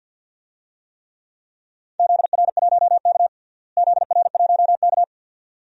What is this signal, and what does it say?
Morse code, heard as a keyed tone, about 49 wpm, 700 Hz. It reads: ZR1K ZR1K
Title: DXCC-100 Call Signs - 4 Characters - Repeated 2x - 50wpm